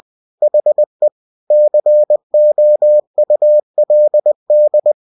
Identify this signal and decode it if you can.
Morse code, heard as a keyed tone, about 20 wpm, 600 Hz. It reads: HE COULD
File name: cw-books / winnie-the-pooh / winnie-20wpm-page-0041